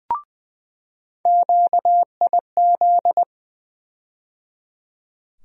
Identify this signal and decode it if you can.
Morse code, heard as a keyed tone, about 20 wpm, 700 Hz. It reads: QIZ
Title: USA Call Sign Postfixes - 3 Letters 20wpm